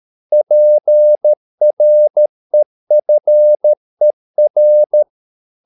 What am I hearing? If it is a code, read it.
Morse code, heard as a keyed tone, about 13 wpm, 600 Hz. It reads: PREFER